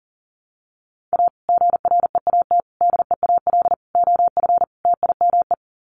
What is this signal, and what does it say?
Morse code, heard as a keyed tone, about 40 wpm, 700 Hz. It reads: A GREAT DEAL OF TIME
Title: Sets of 5 Words Encore2 40wpm